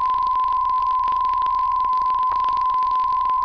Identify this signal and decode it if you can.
PSK63, about 62.5 baud.
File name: UVB33-2016-06-11-1500utc-12025khz-byW2IBC